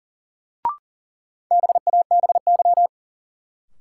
Morse code, heard as a keyed tone, about 37 wpm, 700 Hz. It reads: BABY
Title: Sentences from Top 1300 Words - New Word Drill Alt - 40wpm